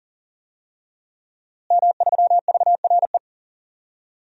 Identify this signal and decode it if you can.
Morse code, heard as a keyed tone, about 40 wpm, 700 Hz. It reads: M3VRE
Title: DXCC-100 Call Signs - 5 Characters - Repeated 3x - 40wpm